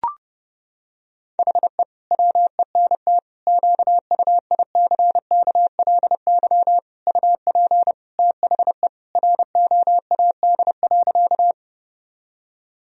Morse code, heard as a keyed tone, about 30 wpm, 700 Hz. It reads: HE WENT QUICKLY UP THE ROAD.
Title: Sentences from Top 1000 Words - Review Part 2 - 30wpm